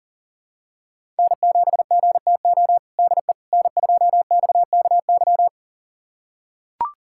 Morse code, heard as a keyed tone, about 40 wpm, 700 Hz. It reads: N7GTO DE N2XKY
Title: USA Callsign DE Callsign - 5 Characters - 40wpm